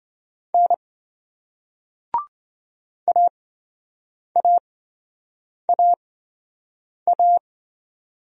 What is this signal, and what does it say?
Morse code, heard as a keyed tone, about 27 wpm, spaced as for 7 wpm, 700 Hz. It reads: N AAAA